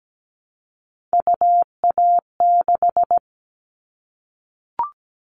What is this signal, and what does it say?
Morse code, heard as a keyed tone, about 17 wpm, 700 Hz. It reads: UA6
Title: Callsign Prefixes for Top 25 DXCC - Repeated 3x - 17wpm